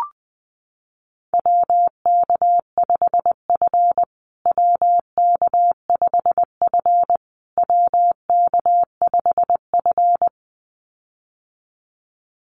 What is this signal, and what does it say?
Morse code, heard as a keyed tone, about 20 wpm, 700 Hz. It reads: WK5F WK5F WK5F
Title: USA Call Signs Repeated 3x 20wpm